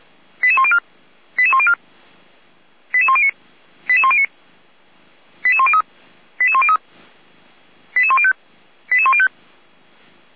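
Five-tone selective call (ZVEI), heard as quick runs of five tones, tones of about 70 ms. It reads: 88104 88104 88109 88109 88103 88103 88105 88105
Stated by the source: ZVEI five-tone selective call received on SDR.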